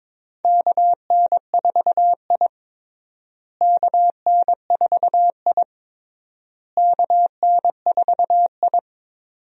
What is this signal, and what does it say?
Morse code, heard as a keyed tone, about 22 wpm, 700 Hz. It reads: KN4I KN4I KN4I
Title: USA Call Signs - 4 Characters - Repeated 3x - 22wpm